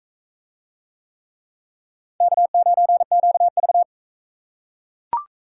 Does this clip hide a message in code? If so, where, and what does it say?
Yes: Morse code from 2.2 to 3.8 s, heard as a keyed tone, about 42 wpm, 700 Hz. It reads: K9QV